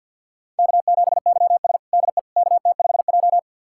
Morse code, heard as a keyed tone, about 46 wpm, 700 Hz. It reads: K7YS DE KT5J
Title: USA Callsign DE Callsign - 4 Characters - 50wpm